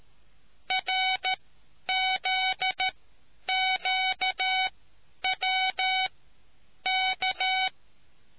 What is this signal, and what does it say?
Morse code, heard as a keyed tone, about 13 wpm, spaced as for 9 wpm, 750 Hz. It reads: RZQWK